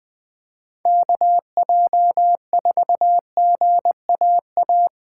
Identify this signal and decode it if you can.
Morse code, heard as a keyed tone, about 20 wpm, 700 Hz. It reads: KJ4GAA
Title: Speed Racing - USA 2x3 Call Signs 17wpm